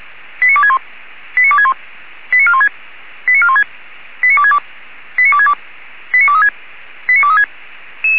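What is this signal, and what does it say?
Five-tone selective call (ZVEI), heard as quick runs of five tones, tones of about 70 ms. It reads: 89261 89261 89416 89416 89262 89262 89236 89236
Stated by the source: ZVEI five-tone selective call 89417 received on SDR.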